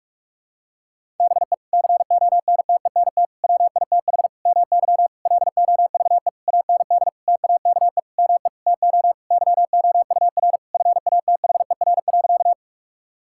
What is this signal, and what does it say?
Morse code, heard as a keyed tone, about 41 wpm, 700 Hz. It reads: BE CONTENT WITH MY LOVE AND TAKE ME TO YOUR FATHER.
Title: Sentences from Top 1000 Words - Review Part 6 - 45wpm